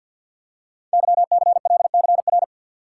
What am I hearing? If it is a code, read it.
Morse code, heard as a keyed tone, about 53 wpm, 700 Hz. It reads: YCLCR
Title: Code Groups - 5 Letters - 50wpm